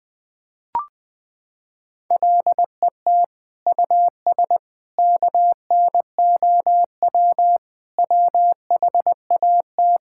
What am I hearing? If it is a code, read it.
Morse code, heard as a keyed tone, about 21 wpm, 700 Hz. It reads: LET US KNOW WHAT
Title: Sentences with 8-Word Phrases - 20wpm